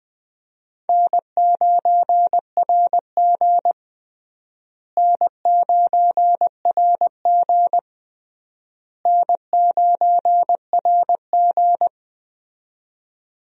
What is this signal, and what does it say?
Morse code, heard as a keyed tone, about 20 wpm, 700 Hz. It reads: N9RG N9RG N9RG
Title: USA Call Signs - 4 Characters - Repeated 3x - 20wpm